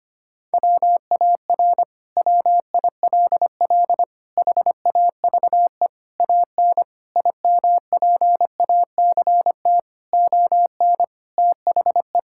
Morse code, heard as a keyed tone, about 25 wpm, 700 Hz. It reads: WAR WILL HAVE AN IMPACT ON THE
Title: Sentences from Top 1000 Words - Review Part 6 - 25wpm